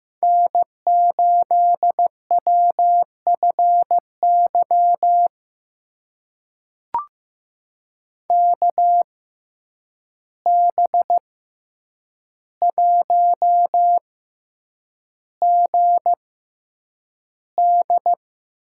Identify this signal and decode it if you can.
Morse code, heard as a keyed tone, about 15 wpm, spaced as for 8 wpm, 700 Hz. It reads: N8WFY KB1GD
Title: USA Call Signs - 6x Character Spacing - 15wpm